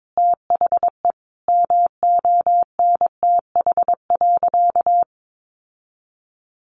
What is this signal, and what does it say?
Morse code, heard as a keyed tone, about 22 wpm, 700 Hz. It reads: THE MONTH.
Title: Sentences from Top 800 Words 22wpm